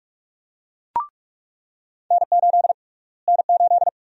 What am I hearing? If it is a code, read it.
Morse code, heard as a keyed tone, about 46 wpm, 700 Hz. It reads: N8 N8
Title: USA Call Sign Prefixes - Repeated 2x - 45wpm